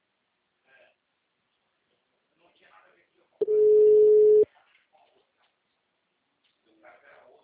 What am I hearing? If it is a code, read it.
Ringback tone (Europe, 425 Hz), with speech.